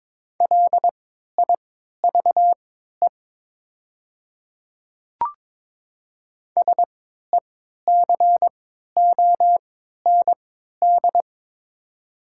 Morse code, heard as a keyed tone, about 22 wpm, spaced as for 13 wpm, 700 Hz. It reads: LIVE SECOND
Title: 3x Character Spacing - Top 300 Words - 22wpm